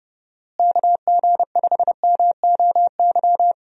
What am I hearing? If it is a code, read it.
Morse code, heard as a keyed tone, about 31 wpm, 700 Hz. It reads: KG5MOY
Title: USA Call Signs - 6 Characters - Repeated 2x - 30wpm